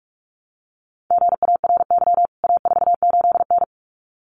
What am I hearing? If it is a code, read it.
Morse code, heard as a keyed tone, about 44 wpm, 700 Hz. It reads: GARY A48N
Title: CWT Exchange 45wpm